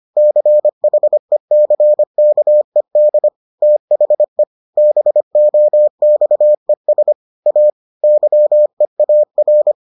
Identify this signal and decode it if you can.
Morse code, heard as a keyed tone, about 25 wpm, 600 Hz. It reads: CHECKED THE BOXES A YEAR